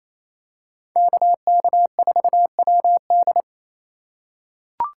Morse code, heard as a keyed tone, about 28 wpm, 700 Hz. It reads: KK4WD
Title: USA Call Signs - 5 Characters - 4x Character Spacing - 28wpm